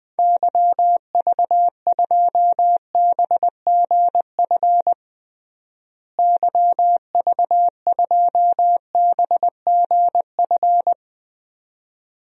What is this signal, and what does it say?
Morse code, heard as a keyed tone, about 20 wpm, 700 Hz. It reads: YV2BGF YV2BGF
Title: DXCC-100 Call Signs - 6 Characters - Repeated 2x - 20wpm